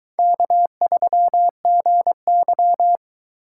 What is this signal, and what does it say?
Morse code, heard as a keyed tone, about 21 wpm, 700 Hz. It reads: K3GY